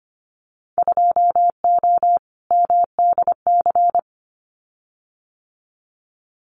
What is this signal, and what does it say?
Morse code, heard as a keyed tone, about 25 wpm, 700 Hz. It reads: 2O MDC
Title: Winter Field Day Exchanges 25wpm